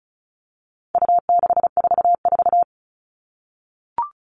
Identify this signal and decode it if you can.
Morse code, heard as a keyed tone, about 34 wpm, 700 Hz. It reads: U644
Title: All Cut Numbers - 4-Digit 35wpm